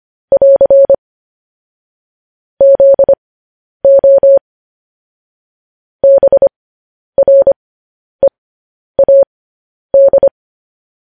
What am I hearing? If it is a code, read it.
Morse code, heard as a keyed tone, about 25 wpm, spaced as for 10 wpm, 550 Hz. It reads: <AR> ZO BREAD
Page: News Headlines in Morse Code at 10 WPM